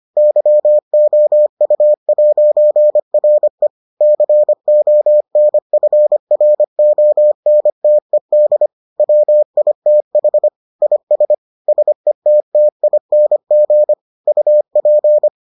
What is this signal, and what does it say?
Morse code, heard as a keyed tone, about 25 wpm, 600 Hz. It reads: YOU'RE CONFRONTED WITH IS SETTING UP